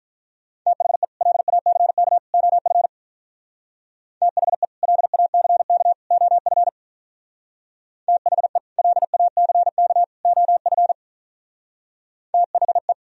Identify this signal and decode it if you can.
Morse code, heard as a keyed tone, about 43 wpm, 700 Hz. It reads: THE LACK OF THE LACK OF THE LACK OF THE
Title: Speed-Racing - Sets of 3 Words 35wpm